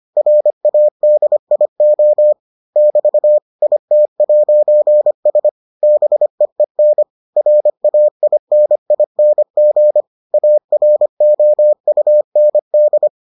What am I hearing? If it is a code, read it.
Morse code, heard as a keyed tone, about 25 wpm, 600 Hz. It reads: RADIO <BT> IT'S BEEN RAINING AROUND